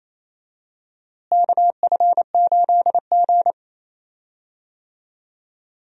Morse code, heard as a keyed tone, about 28 wpm, 700 Hz. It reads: KF8G
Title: USA Call Signs - 4 Characters - Repeated 2x - 28wpm